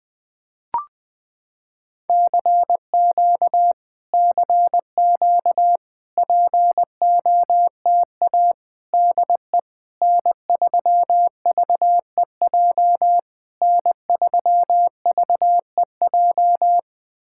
Morse code, heard as a keyed tone, about 20 wpm, 700 Hz. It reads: CQ CQ POTA DE N3VEJ N3VEJ
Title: POTA QSO Practice - 20wpm